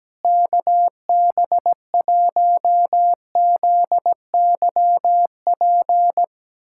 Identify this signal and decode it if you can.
Morse code, heard as a keyed tone, about 17 wpm, 700 Hz. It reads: KB1ZYP
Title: USA Callsign DE Callsign 17wpm